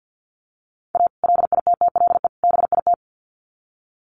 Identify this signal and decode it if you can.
Morse code, heard as a keyed tone, about 50 wpm, 700 Hz. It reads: A LITTLE BIT
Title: Sets of 3 Words 50wpm